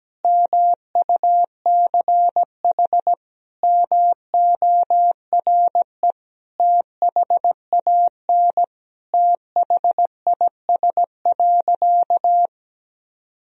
Morse code, heard as a keyed tone, about 17 wpm, 700 Hz. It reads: MUCH MORE THAN THIS.
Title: Sentences from Top 1000 Words - Review Part 1 - 17wpm